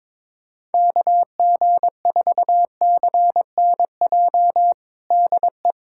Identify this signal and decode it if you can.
Morse code, heard as a keyed tone, about 22 wpm, 700 Hz. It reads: KG4CNJ DE
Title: USA Callsign DE Callsign - 6 Characters - 22wpm